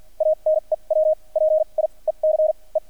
Morse code, heard as a keyed tone, about 50 wpm, spaced as for 33 wpm, 620 Hz. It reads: A NEW PIECE
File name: archived / CRSnet